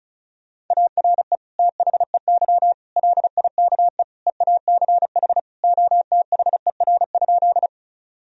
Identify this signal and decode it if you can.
Morse code, heard as a keyed tone, about 34 wpm, 700 Hz. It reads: ARE THEY LIKE EACH OTHER?
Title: Sentences from Top 1000 Words - Review Part 1 - 35wpm